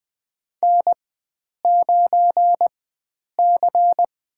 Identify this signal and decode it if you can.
Morse code, heard as a keyed tone, about 20 wpm, spaced as for 9 wpm, 700 Hz. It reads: N9C
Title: USA Call Signs - 4x Character Spacing - 20wpm